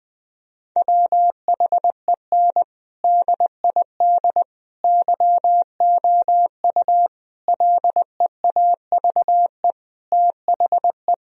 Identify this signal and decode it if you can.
Morse code, heard as a keyed tone, about 20 wpm, 700 Hz. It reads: WHEN DID YOU LEAVE THE
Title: Sentences from Top 1000 Words - Review Part 4 - 20wpm